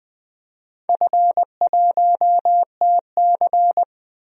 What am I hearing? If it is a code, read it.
Morse code, heard as a keyed tone, about 20 wpm, 700 Hz. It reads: F1TC
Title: DXCC-100 Call Signs - 4 Characters - 20wpm